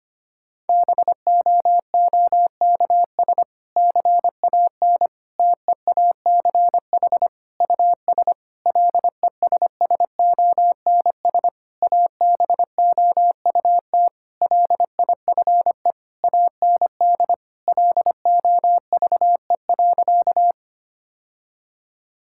Morse code, heard as a keyed tone, about 25 wpm, 700 Hz. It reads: BOOKS CAN TEACH US LESSONS ABOUT LIFE AND LOVE.
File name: practice / 25wpm / Difficult Sentences with Hints - 25wpm